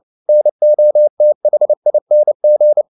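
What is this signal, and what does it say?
Morse code, heard as a keyed tone, about 29 wpm, 600 Hz. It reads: NOTHING
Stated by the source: Morse code at 29 wpm, sending Ken Follett's novel Never in English, from CW med Gnister.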